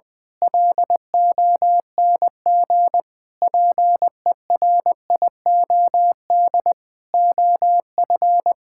Morse code, heard as a keyed tone, about 20 wpm, 700 Hz. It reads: LONG PERIOD OF
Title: Sets of 6 Words - 20wpm